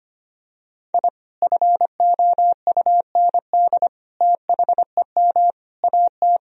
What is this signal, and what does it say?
Morse code, heard as a keyed tone, about 25 wpm, 700 Hz. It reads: I FOUND THEM AT
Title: Sentences from Top 1000 Words - Review Part 2 - 25wpm